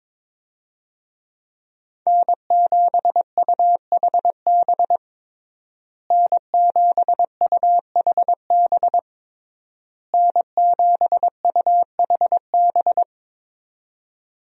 Morse code, heard as a keyed tone, about 22 wpm, 700 Hz. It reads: N7UHB N7UHB N7UHB
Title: USA Call Signs - 5 Characters - Repeated 3x - 22wpm